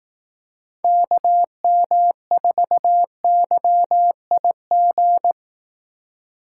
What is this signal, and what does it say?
Morse code, heard as a keyed tone, about 19 wpm, 700 Hz. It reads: KM4YIG
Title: Speed Racing - USA 2x3 Call Signs 15wpm